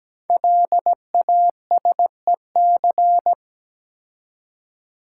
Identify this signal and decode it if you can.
Morse code, heard as a keyed tone, about 17 wpm, 700 Hz. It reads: LASEC
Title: Code Groups - 5 Letters - 17wpm